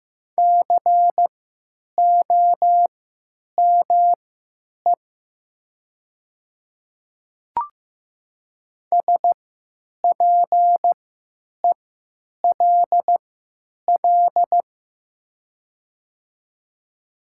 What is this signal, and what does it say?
Morse code, heard as a keyed tone, about 15 wpm, spaced as for 9 wpm, 700 Hz. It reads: COME SPELL
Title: 3x Character Spacing - Top 300 Words - 15wpm